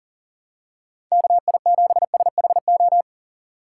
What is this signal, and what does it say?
Morse code, heard as a keyed tone, about 40 wpm, 700 Hz. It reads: KI7SHO
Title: USA Call Signs - 6 Characters - 2x Character Spacing - 40wpm